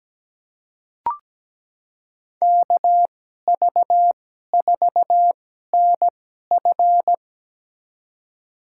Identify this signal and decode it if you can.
Morse code, heard as a keyed tone, about 17 wpm, spaced as for 12 wpm, 700 Hz. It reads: KV4NF
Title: USA Call Signs - 5 Characters - 2x Character Spacing - 17wpm